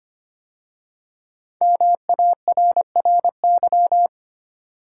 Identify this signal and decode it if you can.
Morse code, heard as a keyed tone, about 25 wpm, 700 Hz. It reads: MARRY
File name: 5 Letter Words - First 3 Letters Given - 25wpm